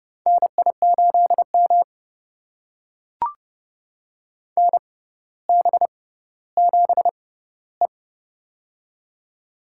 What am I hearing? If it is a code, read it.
Morse code, heard as a keyed tone, about 31 wpm, spaced as for 15 wpm, 700 Hz. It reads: NI8M NB7E
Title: USA Call Signs - 4 Characters - 6x Character Spacing - 30wpm